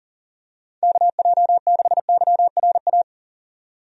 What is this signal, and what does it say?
Morse code, heard as a keyed tone, about 40 wpm, 700 Hz. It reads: KJ6YRA
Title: USA Call Signs - 6 Characters - 40wpm